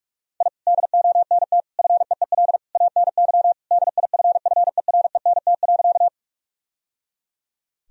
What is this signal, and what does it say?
Morse code, heard as a keyed tone, about 46 wpm, 700 Hz. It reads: I DONT FEEL ANY DIFFERENT.
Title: Sentences from Top 1000 Words - Review Part 3 - 45wpm